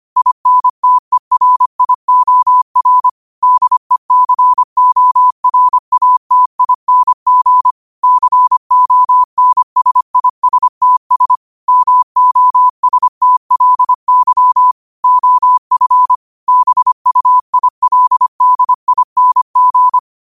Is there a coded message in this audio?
Yes: Morse code from 0.2 to 20.0 s, heard as a keyed tone, about 25 wpm, 1000 Hz. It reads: INTERIOR DECORATING CONSISTS MOSTLY OF BUILDING